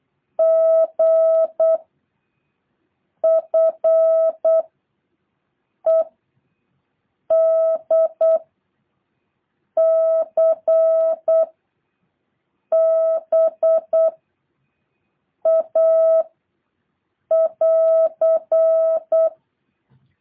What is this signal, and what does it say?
Morse code, heard as a keyed tone, about 8 wpm, 650 Hz. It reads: G F E D C B A <AR>
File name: jh8cbh